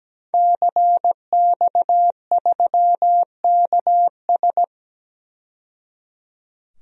Morse code, heard as a keyed tone, about 17 wpm, 700 Hz. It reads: CX3KS
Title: DXCC-100 Call Signs - 5 Characters - 17wpm